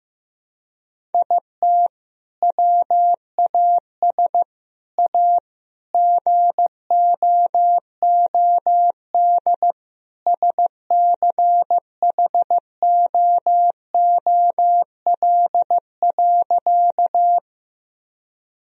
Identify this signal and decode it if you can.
Morse code, heard as a keyed tone, about 16 wpm, 700 Hz. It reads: IT WAS A GOOD SCHOOL.
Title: Sentences from Top 1000 Words - Review Part 1 - 15wpm